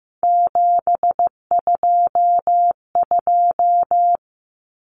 Morse code, heard as a keyed tone, about 15 wpm, 700 Hz. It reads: 722
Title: Common Cut Numbers - 3-Digit 15wpm